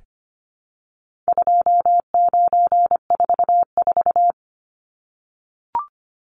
Morse code, heard as a keyed tone, about 25 wpm, 700 Hz. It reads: 2944
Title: Numbers - 4-Digit 25wpm